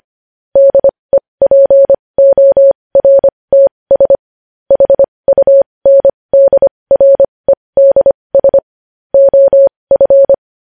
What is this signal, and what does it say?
Morse code, heard as a keyed tone, about 25 wpm, spaced as for 20 wpm, 550 Hz. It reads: DEPORTS HUNDREDS OF
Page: News Headlines in Morse Code at 20 WPM